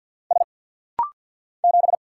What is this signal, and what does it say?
Morse code, heard as a keyed tone, about 50 wpm, 700 Hz. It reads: S 7
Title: Single Letter-Number - Rapid-Fire 50wpm